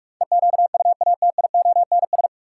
Morse code, heard as a keyed tone, about 41 wpm, 700 Hz. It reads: EQUATIONS
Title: Sets of 5 Words - New Word Drill - 45wpm